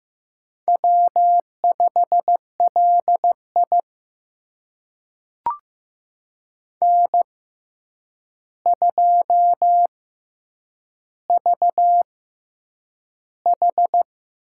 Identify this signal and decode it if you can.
Morse code, heard as a keyed tone, about 15 wpm, spaced as for 8 wpm, 700 Hz. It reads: W5LI N2VH
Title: USA Call Signs - 4 Characters - 6x Character Spacing - 15wpm